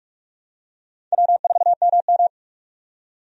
Morse code, heard as a keyed tone, about 46 wpm, 700 Hz. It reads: W4MM
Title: USA Call Signs - 4 Characters - 45wpm